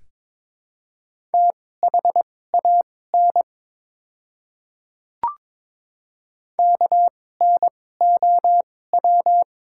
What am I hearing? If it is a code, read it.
Morse code, heard as a keyed tone, about 22 wpm, spaced as for 16 wpm, 700 Hz. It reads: THAN KNOW
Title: 2x Character Spacing - Top 100 Words - 22wpm